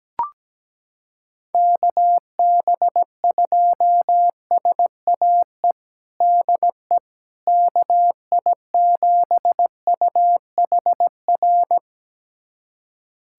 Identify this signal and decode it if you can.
Morse code, heard as a keyed tone, about 17 wpm, 700 Hz. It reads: KB2SAE DE KI7UHR
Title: USA Callsign DE Callsign - 6 Characters - 17wpm